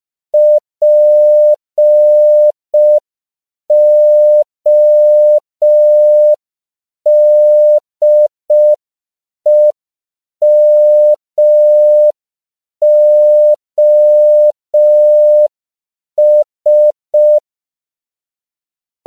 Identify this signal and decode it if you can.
Morse code, heard as a keyed tone, about 5 wpm, 600 Hz. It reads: PODEMOS